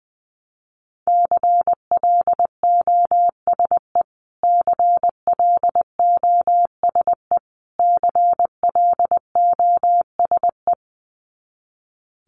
Morse code, heard as a keyed tone, about 20 wpm, 700 Hz. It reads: CLOSE CLOSE CLOSE
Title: Top 300 Words Repeated 20wpm